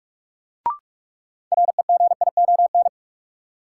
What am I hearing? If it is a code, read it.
Morse code, heard as a keyed tone, about 46 wpm, 700 Hz. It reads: REGION
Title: Sentences from Top 1500 Words - New Word Drill - 45wpm